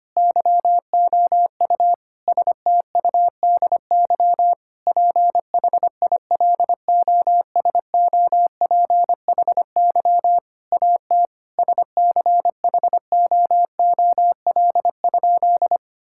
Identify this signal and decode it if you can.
Morse code, heard as a keyed tone, about 25 wpm, 700 Hz. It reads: YOU STUDY PHILOSOPHY AT SCHOOL?
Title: Sentences from Top 1000 Words - Review Part 6 - 25wpm